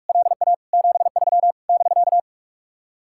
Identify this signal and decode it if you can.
Morse code, heard as a keyed tone, about 55 wpm, 700 Hz. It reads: LA 73 <BK>
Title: Speed-Racing Common QSO Phrases - 45wpm